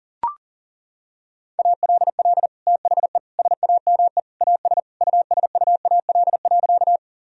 Morse code, heard as a keyed tone, about 40 wpm, 700 Hz. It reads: ALL THE SAME AS USUAL.
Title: Sentences from Top 1000 Words - Review Part 2 - 40wpm